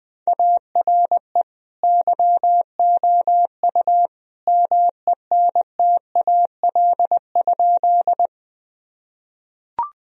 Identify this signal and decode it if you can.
Morse code, heard as a keyed tone, about 20 wpm, 700 Hz. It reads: ARE YOU MENTAL?
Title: Sentences from Top 1000 Words - Review Part 4 - 20wpm